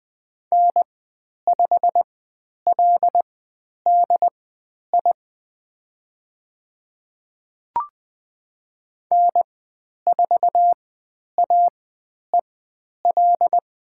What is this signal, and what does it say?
Morse code, heard as a keyed tone, about 20 wpm, spaced as for 10 wpm, 700 Hz. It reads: N5LDI N4AEL
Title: 102 Lesson 17g - Letter u - Call Signs 20wpm